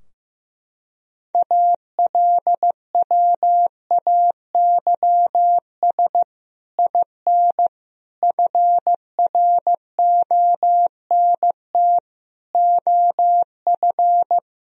Morse code, heard as a keyed tone, about 16 wpm, 700 Hz. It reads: ALWAYS IN FRONT OF
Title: Sentences from Top 1000 Words - Review Part 2 - 15wpm